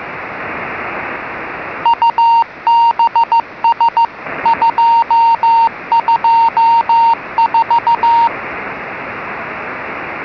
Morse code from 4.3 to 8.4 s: 224